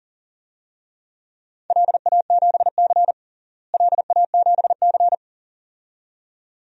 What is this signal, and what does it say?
Morse code, heard as a keyed tone, about 37 wpm, 700 Hz. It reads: LA7C LA7C
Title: DXCC-100 Call Signs - 4 Characters - Repeated 2x - 40wpm